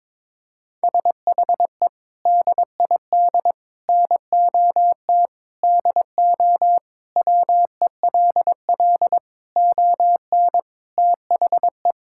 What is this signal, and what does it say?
Morse code, heard as a keyed tone, about 21 wpm, 700 Hz. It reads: SHE DID NOT DO WELL ON THE
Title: Sentences from Top 1000 Words - Review Part 5 - 22wpm